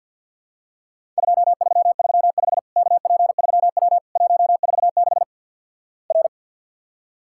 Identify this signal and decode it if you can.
Morse code, heard as a keyed tone, about 52 wpm, 700 Hz. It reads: 2335 KP3W 146 R
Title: Worked All Europe DX Contest - QTC Exchange - 50wpm